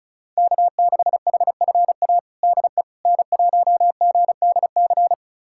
Morse code, heard as a keyed tone, about 34 wpm, 700 Hz. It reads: K6HFA DE N1GDC
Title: USA Callsign DE Callsign - 5 Characters - 35wpm